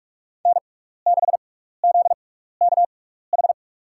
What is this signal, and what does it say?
Morse code, heard as a keyed tone, about 46 wpm, spaced as for 15 wpm, 700 Hz. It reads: N6ZKH
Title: USA Call Signs - 6x Character Spacing - 45wpm